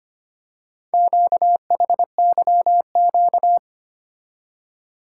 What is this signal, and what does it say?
Morse code, heard as a keyed tone, about 25 wpm, 700 Hz. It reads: QHYQ
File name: practice / 25wpm / Four Letter-Number Combinations - 25wpm